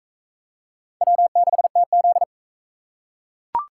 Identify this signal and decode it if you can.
Morse code, heard as a keyed tone, about 37 wpm, 700 Hz. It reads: W6TZ